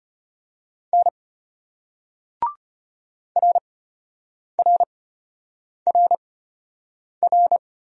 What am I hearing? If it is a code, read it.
Morse code, heard as a keyed tone, about 31 wpm, spaced as for 8 wpm, 700 Hz. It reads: N RRRR
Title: Speed-Racing Single Letters 25wpm